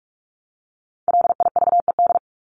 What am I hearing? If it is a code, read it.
Morse code, heard as a keyed tone, about 44 wpm, 700 Hz. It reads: LIVED
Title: ED Suffix - Top 100 Words with Base Words 45wpm